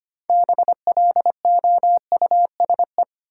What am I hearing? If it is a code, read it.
Morse code, heard as a keyed tone, about 25 wpm, 700 Hz. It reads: BLOUSE
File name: practice / 25wpm / Difficult Words with Hints - 25wpm